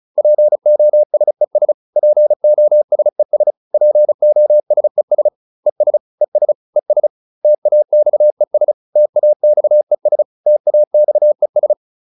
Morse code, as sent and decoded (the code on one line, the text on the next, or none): .--. --- ... ... / .--. --- ... ... / .--. --- ... ... / . ... / . ... / . ... / - .- -..- . ... / - .- -..- . ... / - .- -..- . ...
POSES POSES POSES ES ES ES TAXES TAXES TAXES